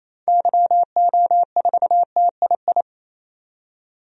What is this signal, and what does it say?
Morse code, heard as a keyed tone, about 28 wpm, 700 Hz. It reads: YO4TII
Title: DXCC-100 Call Signs - 6 Characters - 28wpm